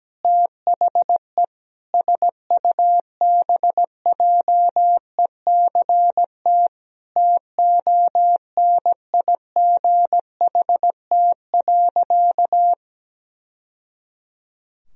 Morse code, heard as a keyed tone, about 17 wpm, 700 Hz. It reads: THE SUBJECT TONIGHT.
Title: Sentences from Top 1000 Words - Review Part 4 - 17wpm